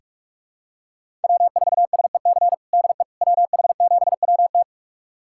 Morse code, heard as a keyed tone, about 46 wpm, 700 Hz. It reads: W4SEC DE WH7WT
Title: USA Callsign DE Callsign - 5 Characters - 45wpm